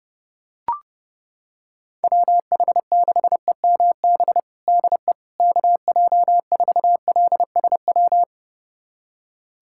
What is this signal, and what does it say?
Morse code, heard as a keyed tone, about 30 wpm, 700 Hz. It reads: WH6EMB DE KJ4LSW
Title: USA Callsign DE Callsign 30wpm